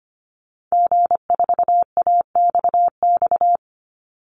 Morse code, heard as a keyed tone, about 25 wpm, 700 Hz. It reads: G4AXX
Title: Call Signs - 25wpm